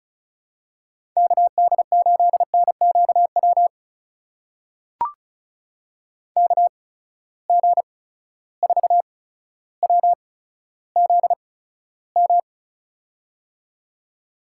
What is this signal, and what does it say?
Morse code, heard as a keyed tone, about 35 wpm, spaced as for 15 wpm, 700 Hz. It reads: KD8NQW KG4WZM
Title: USA Call Signs - 6 Characters - 8x Character Spacing - 35wpm